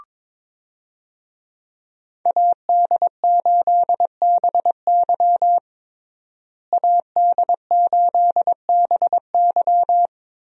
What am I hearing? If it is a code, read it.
Morse code, heard as a keyed tone, about 21 wpm, 700 Hz. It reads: AD8BY AD8BY
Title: USA Call Signs - 5 Characters - Repeated 2x - 22wpm